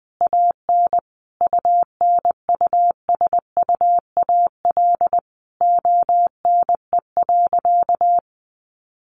Morse code, heard as a keyed tone, about 20 wpm, 700 Hz. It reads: AN UNUSUAL ONE.
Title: Sentences from Top 1400 Words 20wpm